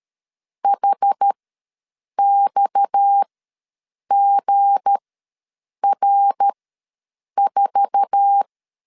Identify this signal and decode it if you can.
Morse code, heard as a keyed tone, about 13 wpm, spaced as for 7 wpm, 800 Hz. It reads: HXGR4
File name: LEZIONI / 10 SIGNS / 10C DOT ws 12wpm 3548